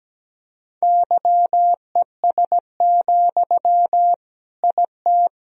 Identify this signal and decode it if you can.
Morse code, heard as a keyed tone, about 17 wpm, 700 Hz. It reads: YES, IT
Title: Sentences from Top 1000 Words - Review Part 1 - 17wpm